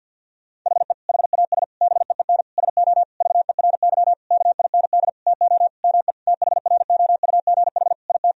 Morse code, heard as a keyed tone, about 50 wpm, 700 Hz. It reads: HE HAS BEEN SO VERY KIND TO ME THROUGH IT
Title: Sentences from Top 1000 Words - Review Part 1 - 50wpm